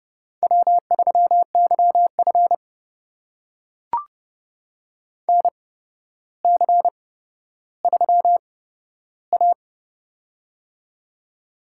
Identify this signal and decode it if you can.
Morse code, heard as a keyed tone, about 32 wpm, spaced as for 13 wpm, 700 Hz. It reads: W3YF NC3A
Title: USA Call Signs - 4 Characters - 8x Character Spacing - 30wpm